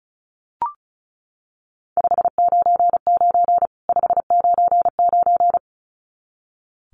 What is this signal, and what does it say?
Morse code, heard as a keyed tone, about 34 wpm, 700 Hz. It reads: 599 599
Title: Common QSO Phrases 35wpm